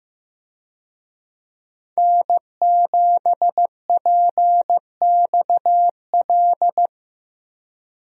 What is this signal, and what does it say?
Morse code, heard as a keyed tone, about 15 wpm, 700 Hz. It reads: N7PXL
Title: USA Call Signs - 5 Characters - Repeated 3x - 15wpm